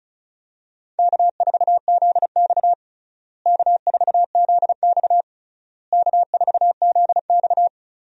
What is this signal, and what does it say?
Morse code, heard as a keyed tone, about 36 wpm, 700 Hz. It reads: K4ZX K4ZX K4ZX
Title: USA Call Signs - 4 Characters - Repeated 3x - 35wpm